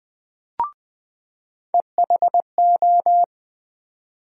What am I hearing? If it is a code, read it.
Morse code, heard as a keyed tone, about 20 wpm, 700 Hz. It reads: EHO
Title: USA Call Sign Postfixes - 3 Letters 20wpm